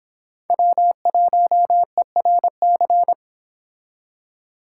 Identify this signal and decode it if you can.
Morse code, heard as a keyed tone, about 30 wpm, 700 Hz. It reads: W1ERC